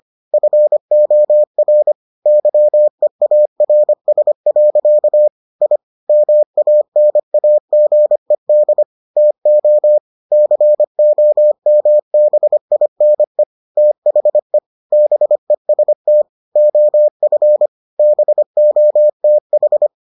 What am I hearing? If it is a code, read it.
Morse code, heard as a keyed tone, about 25 wpm, 600 Hz. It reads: FOR YEARS. I MANAGED TO COMBINE THE BEST OF BOTH